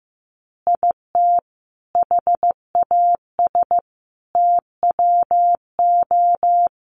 Morse code, heard as a keyed tone, about 15 wpm, 700 Hz. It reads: IT HAS TWO
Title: Additional Easy Phrases 15wpm